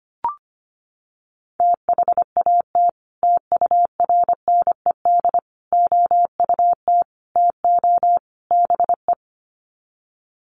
Morse code, heard as a keyed tone, about 25 wpm, 700 Hz. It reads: THAT TURNED OUT TO BE
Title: Sets of 5 Words Encore2 25wpm